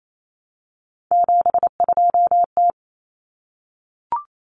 Morse code, heard as a keyed tone, about 28 wpm, 700 Hz. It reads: 72T
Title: Common Cut Numbers - 3-Digit 28wpm